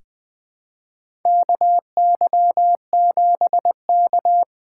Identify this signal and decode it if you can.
Morse code, heard as a keyed tone, about 20 wpm, 700 Hz. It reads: KY7K
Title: K1USN SST QSO Practice - 20wpm